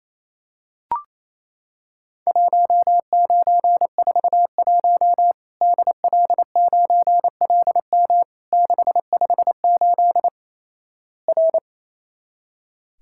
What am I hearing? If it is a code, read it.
Morse code, heard as a keyed tone, about 28 wpm, 700 Hz. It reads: 1941 DL9LM 658 R
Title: Worked All Europe DX Contest - QTC Exchange - 28wpm